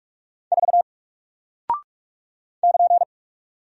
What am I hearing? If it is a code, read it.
Morse code, heard as a keyed tone, about 46 wpm, 700 Hz. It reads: 4 <KN>
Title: Single Character (Letter:Number:Punctuation:Prosign) 45wpm